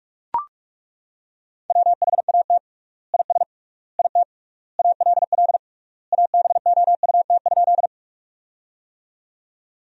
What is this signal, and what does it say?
Morse code, heard as a keyed tone, about 41 wpm, 700 Hz. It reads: WHAT IS IT ALL ABOUT?
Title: 3x Word-Spacing - Sentences from Top 100 Words - 45wpm